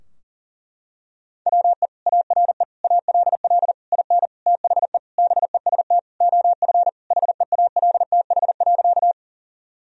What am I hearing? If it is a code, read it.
Morse code, heard as a keyed tone, about 40 wpm, 700 Hz. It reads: WE ARE ALL IN THE BEST OF HEALTH.
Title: Sentences from Top 1000 Words - Review Part 3 - 40wpm